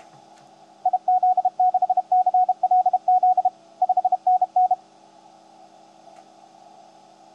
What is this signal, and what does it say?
Morse code, heard as a keyed tone, about 32 wpm, 750 Hz. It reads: IZ6CLZ 5NN
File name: tt8tt-10m-cw-here-in-jn90ab